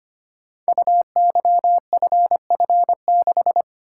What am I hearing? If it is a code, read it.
Morse code, heard as a keyed tone, about 25 wpm, 700 Hz. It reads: UYFF6
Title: Code Groups - 5 Letter:Numbers - 25wpm